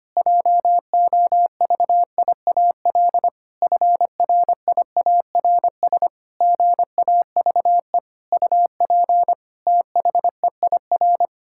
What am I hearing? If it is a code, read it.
Morse code, heard as a keyed tone, about 25 wpm, 700 Hz. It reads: JOVIAL FRIARS GAVE UP THEIR
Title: Pangram Challenge v2 - 25wpm